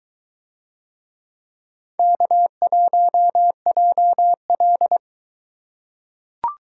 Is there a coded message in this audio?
Yes: Morse code from 2.0 to 5.0 s, heard as a keyed tone, about 23 wpm, 700 Hz. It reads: K1JL